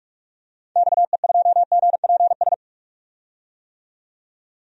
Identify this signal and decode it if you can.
Morse code, heard as a keyed tone, about 41 wpm, 700 Hz. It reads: XE2GPS
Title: DXCC-100 Call Signs - 6 Characters - 45wpm